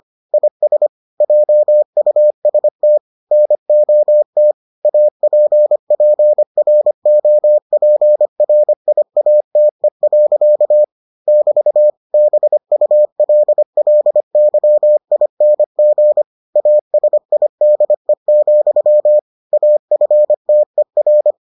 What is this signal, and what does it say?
Morse code, heard as a keyed tone, about 25 wpm, 600 Hz. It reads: IS JUST NOT APPROPRIATE. <BT> BULLYING ASIDE, AFTER